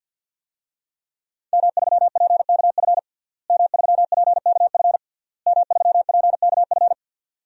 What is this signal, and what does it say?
Morse code, heard as a keyed tone, about 51 wpm, 700 Hz. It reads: M3PKF M3PKF M3PKF
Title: DXCC-100 Call Signs - 5 Characters - Repeated 3x - 50wpm